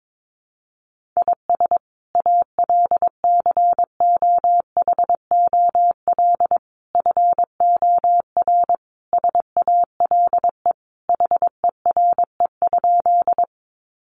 Morse code, heard as a keyed tone, about 22 wpm, 700 Hz. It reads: IS ALCOHOL FOR SALE HERE?
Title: Sentences from Top 1400 Words 22wpm